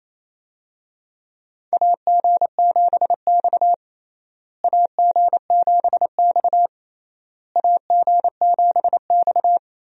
Morse code, heard as a keyed tone, about 28 wpm, 700 Hz. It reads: AG7X AG7X AG7X
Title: USA Call Signs - 4 Characters - Repeated 3x - 28wpm